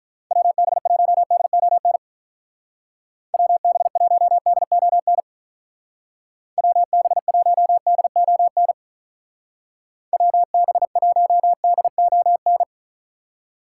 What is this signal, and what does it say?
Morse code, heard as a keyed tone, about 43 wpm, 700 Hz. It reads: WB1DON WB1DON WB1DON WB1DON
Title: Speed Racing - USA 2x3 Call Signs 35wpm